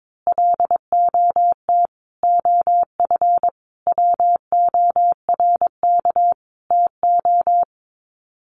Morse code, heard as a keyed tone, about 22 wpm, 700 Hz. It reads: LOT OF WORK TO
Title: Sets of 5 Words Encore2 22wpm